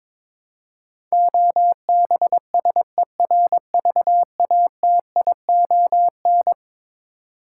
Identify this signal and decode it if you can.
Morse code, heard as a keyed tone, about 22 wpm, 700 Hz. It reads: OBSERVATION
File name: practice / 22wpm / Eleven Letter Words - 22wpm